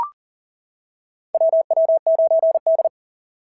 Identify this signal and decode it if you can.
Morse code, heard as a keyed tone, about 37 wpm, 690 Hz. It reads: WW9D
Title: SOTA QSO Practice 40wpm